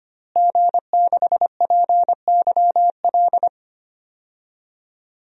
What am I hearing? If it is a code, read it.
Morse code, heard as a keyed tone, about 25 wpm, 700 Hz. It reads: G6PYL
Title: DXCC-100 Call Signs - 5 Characters - Repeated 2x - 25wpm